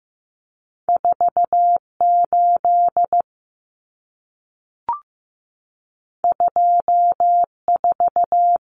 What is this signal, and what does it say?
Morse code, heard as a keyed tone, about 15 wpm, 700 Hz. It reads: 48 24
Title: Numbers - 2-Digit 15wpm